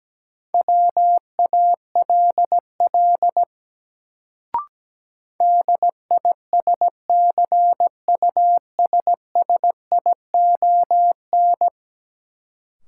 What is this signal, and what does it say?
Morse code, heard as a keyed tone, about 17 wpm, 700 Hz. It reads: WALL DISCUSSION
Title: Sentences from Top 1100 Words - New Word Drill Alt - 17wpm